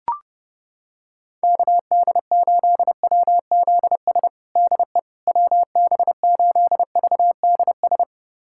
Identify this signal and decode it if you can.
Morse code, heard as a keyed tone, about 30 wpm, 700 Hz. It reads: KD8WZS DE WB8VDS
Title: USA Callsign DE Callsign - 6 Characters - 30wpm